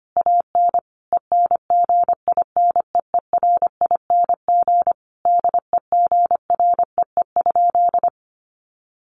Morse code, heard as a keyed tone, about 25 wpm, 700 Hz. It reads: AN ENGINEERING DEGREE?
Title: Sentences from Top 1500 Words 25wpm